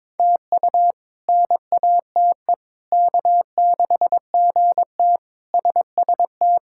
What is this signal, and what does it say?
Morse code, heard as a keyed tone, about 22 wpm, 700 Hz. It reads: TU NATE K6GT SST
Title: K1USN SST QSO Practice - 22wpm